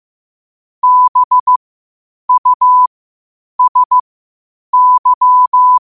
Morse code, heard as a keyed tone, about 15 wpm, spaced as for 8 wpm, 1000 Hz. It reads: BUSY
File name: qotdfiles / 8wpm